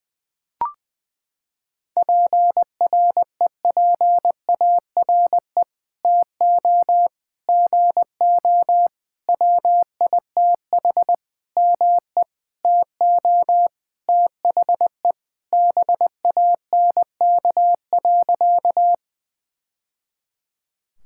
Morse code, heard as a keyed tone, about 20 wpm, 700 Hz. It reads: PREPARE TO GO WITH ME TO THE BANK.
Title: Sentences from Top 1000 Words - Review Part 6 - 20wpm